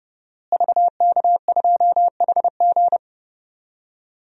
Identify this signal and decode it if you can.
Morse code, heard as a keyed tone, about 30 wpm, 700 Hz. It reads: VK2HG
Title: DXCC-100 Call Signs - 5 Characters - Repeated 3x - 30wpm